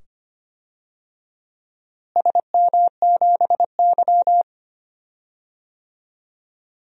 Morse code, heard as a keyed tone, about 25 wpm, 700 Hz. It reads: SM7Y
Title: DXCC-100 Call Signs - 4 Characters - Repeated 2x - 25wpm